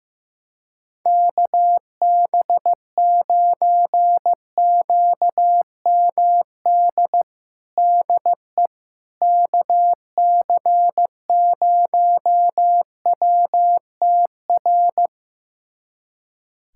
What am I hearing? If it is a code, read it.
Morse code, heard as a keyed tone, about 15 wpm, 700 Hz. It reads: KB9QMD DE KC0WTR
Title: USA Callsign DE Callsign 15wpm